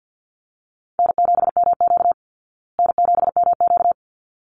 Morse code, heard as a keyed tone, about 50 wpm, 700 Hz. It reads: N7MQ N7MQ
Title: Easy Call Signs Repeated 2x - 50wpm